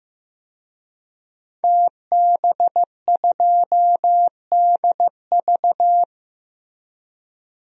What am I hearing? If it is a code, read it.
Morse code, heard as a keyed tone, about 15 wpm, 700 Hz. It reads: TB2DV
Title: DXCC-100 Call Signs - 5 Characters - Repeated 2x - 15wpm